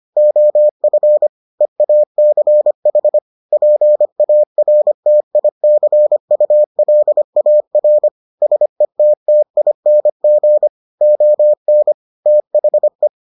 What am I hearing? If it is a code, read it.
Morse code, heard as a keyed tone, about 25 wpm, 600 Hz. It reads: OF EACH PARTICULAR SETTING ON THE